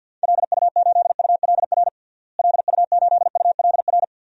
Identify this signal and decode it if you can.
Morse code, heard as a keyed tone, about 52 wpm, 700 Hz. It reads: LU8ULR LU8ULR
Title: DXCC-100 Call Signs - 6 Characters - Repeated 3x - 50wpm